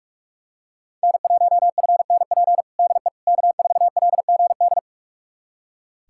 Morse code, heard as a keyed tone, about 42 wpm, 700 Hz. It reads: N1FNP DE K4LGD
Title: USA Callsign DE Callsign - 5 Characters - 45wpm